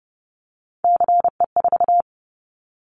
Morse code, heard as a keyed tone, about 30 wpm, 700 Hz. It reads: CE4
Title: Callsign Prefixes for Top 100 DXCC - 30wpm